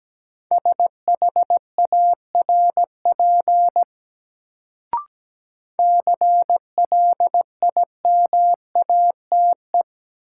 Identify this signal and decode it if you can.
Morse code, heard as a keyed tone, about 17 wpm, 700 Hz. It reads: SHARP CLIMATE
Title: Sentences from Top 1600 Words - New Word Drill Alt - 17wpm